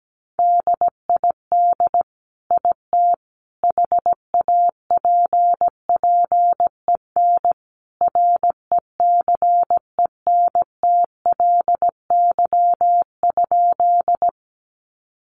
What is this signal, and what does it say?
Morse code, heard as a keyed tone, about 17 wpm, 700 Hz. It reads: DID IT HAPPEN RECENTLY?
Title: Sentences from Top 1200 Words 17wpm